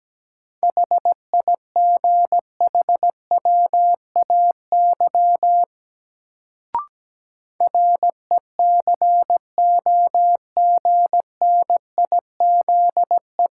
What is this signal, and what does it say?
Morse code, heard as a keyed tone, about 17 wpm, 700 Hz. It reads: HIGHWAY RECOGNIZE
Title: Sentences from Top 1400 Words - New Word Drill Alt - 17wpm